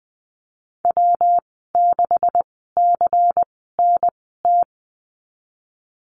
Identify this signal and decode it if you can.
Morse code, heard as a keyed tone, about 20 wpm, spaced as for 15 wpm, 700 Hz. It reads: W6CNT
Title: 161 Lesson 26g - Letter g - Call Signs 20wpm